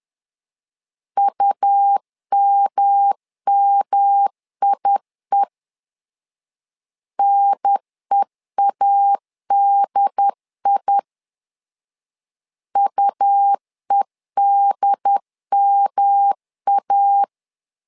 Morse code, heard as a keyed tone, about 11 wpm, 800 Hz. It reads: UMMIE NEADI UEDMA
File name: LEZIONI / 3 MWRO / 3 M 10wpm 1828